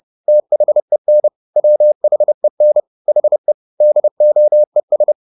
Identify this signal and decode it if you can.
Morse code, heard as a keyed tone, about 30 wpm, 600 Hz. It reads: THEN WHEN HE DOES
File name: the old man and the sea / the old man and the sea 1 30wpm